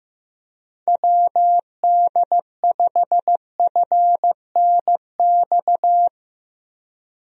Morse code, heard as a keyed tone, about 15 wpm, 700 Hz. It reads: WD5FNX
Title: USA Call Signs - 6 Characters - Repeated 2x - 15wpm